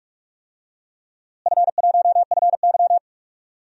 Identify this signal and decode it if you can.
Morse code, heard as a keyed tone, about 46 wpm, 700 Hz. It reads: F1FY
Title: DXCC-100 Call Signs - 4 Characters - Repeated 3x - 45wpm